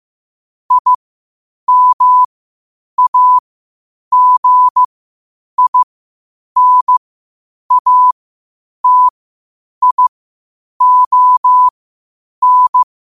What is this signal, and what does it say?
Morse code, heard as a keyed tone, about 15 wpm, spaced as for 8 wpm, 1000 Hz. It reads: IMAGINATION
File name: qotdfiles / 8wpm